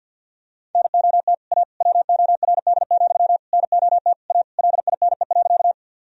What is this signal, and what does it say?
Morse code, heard as a keyed tone, about 51 wpm, 700 Hz. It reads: NOT A WORD, NOT A LINE.
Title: Sentences from Top 1000 Words - Review Part 1 - 50wpm